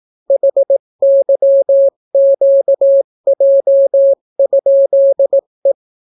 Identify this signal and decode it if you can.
Morse code, heard as a keyed tone, about 18 wpm, 550 Hz. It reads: HYQJ?E